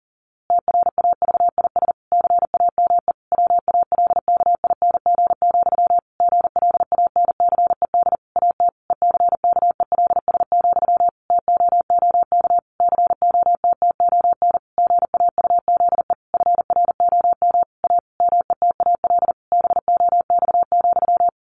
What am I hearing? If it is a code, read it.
Morse code, heard as a keyed tone, about 37 wpm, 700 Hz. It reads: TRAVIS CAME WALKING, GLANCED AT ECKELS, TOOK COTTON GAUZE FROM A METAL BOX,
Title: A Sound Of Thunder 40wpm